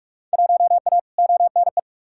Morse code, heard as a keyed tone, about 43 wpm, 700 Hz. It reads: 1A ONE
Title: Field Day Exchanges - 45wpm